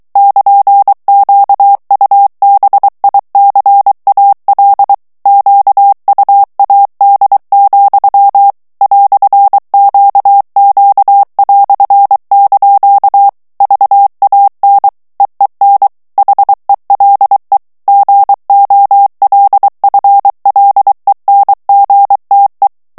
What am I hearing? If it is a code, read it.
Morse code, heard as a keyed tone, about 23 wpm, 800 Hz. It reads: <KN>QUBICAL QUAD, "QQ") VAN EEN HELE GOLFLENGTE
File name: HOLLANDAIS texte clair 1 vitesse 22 mots minute